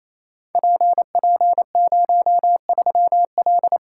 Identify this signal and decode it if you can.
Morse code, heard as a keyed tone, about 28 wpm, 700 Hz. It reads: PP03L
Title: Code Groups - 5 Letter:Numbers - 28wpm